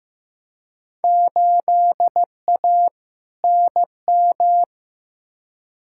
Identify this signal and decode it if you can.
Morse code, heard as a keyed tone, about 15 wpm, 700 Hz. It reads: 8A NM
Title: Field Day Exchanges Encore - 15wpm